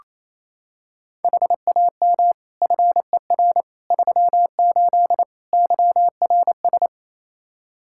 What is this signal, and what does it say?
Morse code, heard as a keyed tone, about 28 wpm, 700 Hz. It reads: HAM FER 38 YRS
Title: K1USN SST QSO Practice 28wpm